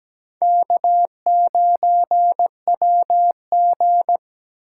Morse code, heard as a keyed tone, about 17 wpm, 700 Hz. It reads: K9WG
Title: USA Call Signs - 4 Characters - 8x Character Spacing - 17wpm